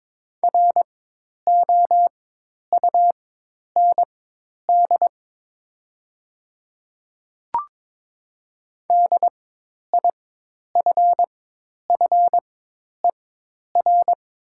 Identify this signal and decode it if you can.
Morse code, heard as a keyed tone, about 22 wpm, spaced as for 10 wpm, 700 Hz. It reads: ROUND DIFFER
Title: 4x Character Spacing - Top 300 Words - 22wpm